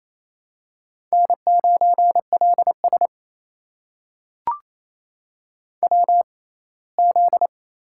Morse code, heard as a keyed tone, about 28 wpm, spaced as for 19 wpm, 700 Hz. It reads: N9LS WZ
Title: USA Call Signs - 6x Character Spacing - 28wpm